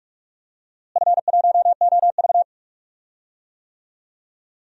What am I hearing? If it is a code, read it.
Morse code, heard as a keyed tone, about 46 wpm, 700 Hz. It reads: F1OV
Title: DXCC-100 Call Signs - 4 Characters - Repeated 3x - 45wpm